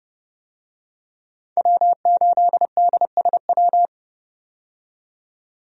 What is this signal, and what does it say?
Morse code, heard as a keyed tone, about 30 wpm, 700 Hz. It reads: W8DSW
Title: USA Call Signs - 5 Characters - 30wpm